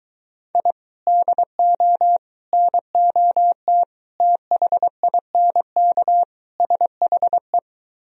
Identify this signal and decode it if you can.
Morse code, heard as a keyed tone, about 22 wpm, 700 Hz. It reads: I DO NOT THINK SHE